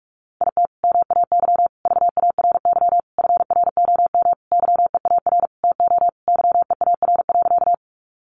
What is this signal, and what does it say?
Morse code, heard as a keyed tone, about 46 wpm, 700 Hz. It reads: IT MAY VARY FROM YEAR TO YEAR.
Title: Sentences from Top 1200 Words 45wpm